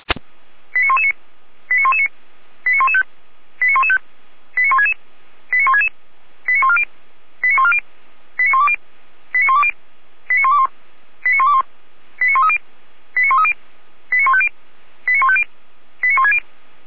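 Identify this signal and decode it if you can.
Five-tone selective call (ZVEI), heard as quick runs of five tones, tones of about 70 ms. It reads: 89119 89119 89115 89115 89166 89166 89140 89140 89120 89120 89121 89121 89130 89130 89150 89150 89160
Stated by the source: ZVEI five-tone selective call received on an SDR.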